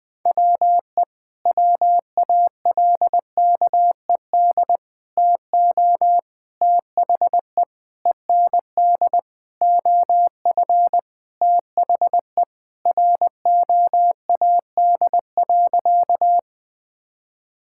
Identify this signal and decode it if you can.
Morse code, heard as a keyed tone, about 21 wpm, 700 Hz. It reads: WE WALKED TO THE END OF THE ROAD.
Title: Sentences with 5-Word Phrases - Spoken Beginning - 20wpm